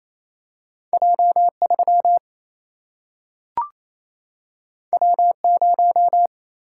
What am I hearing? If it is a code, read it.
Morse code, heard as a keyed tone, about 28 wpm, 700 Hz. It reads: J3 W0
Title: Single-Letter Single-Number 28wpm